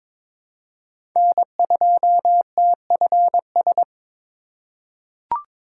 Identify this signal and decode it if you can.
Morse code, heard as a keyed tone, about 22 wpm, 700 Hz. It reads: N2TFS
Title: USA Call Signs - 5 Characters - 4x Character Spacing - 22wpm